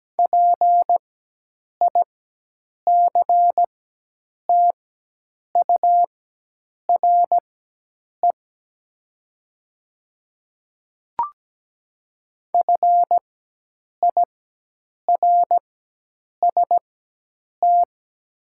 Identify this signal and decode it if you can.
Morse code, heard as a keyed tone, about 17 wpm, spaced as for 8 wpm, 700 Hz. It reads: PICTURE FIRST
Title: 4x Character Spacing - Top 200 Words - 17wpm